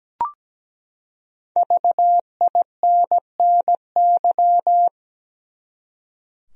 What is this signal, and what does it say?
Morse code, heard as a keyed tone, about 17 wpm, 700 Hz. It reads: VINNY
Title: USA Common Names - 17wpm